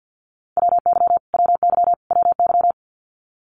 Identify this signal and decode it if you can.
Morse code, heard as a keyed tone, about 50 wpm, 700 Hz. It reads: WY WY WY
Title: US States Repeated 3x 50wpm